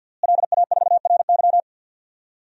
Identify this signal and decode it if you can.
Morse code, heard as a keyed tone, about 50 wpm, 700 Hz. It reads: LA4RY
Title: DXCC-100 Call Signs - 5 Characters - Repeated 3x - 50wpm